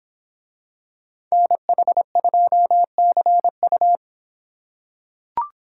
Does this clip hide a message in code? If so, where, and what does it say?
Yes: Morse code from 1.3 to 4.0 s, heard as a keyed tone, about 24 wpm, 700 Hz. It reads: NH2CU